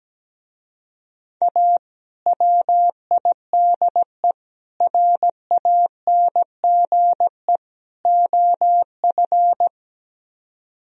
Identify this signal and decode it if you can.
Morse code, heard as a keyed tone, about 17 wpm, 700 Hz. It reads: A WIDE RANGE OF
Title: Speed-Racing - Sets of 4 Words 17wpm